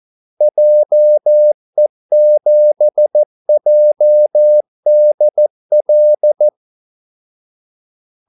Morse code, heard as a keyed tone, about 14 wpm, 600 Hz. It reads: JE7JDL
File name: Callsigns-14wpm